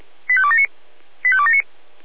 Five-tone selective call (ZVEI), heard as quick runs of five tones, tones of about 70 ms. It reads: 85279 85279